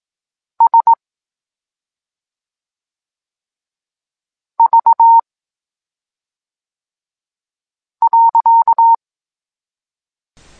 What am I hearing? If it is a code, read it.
Morse code, heard as a keyed tone, about 18 wpm, 920 Hz. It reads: S V .
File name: Tekst-9-rownomierny